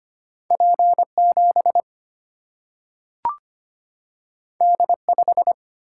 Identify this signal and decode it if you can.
Morse code, heard as a keyed tone, about 25 wpm, 700 Hz. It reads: P 7 D 5
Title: Single-Letter Single-Number 25wpm